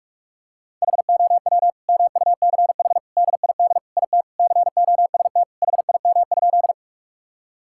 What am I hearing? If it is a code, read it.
Morse code, heard as a keyed tone, about 41 wpm, 700 Hz. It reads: HOW MUCH DID IT COST HIM?
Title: Sentences from Top 1000 Words - Review Part 5 - 45wpm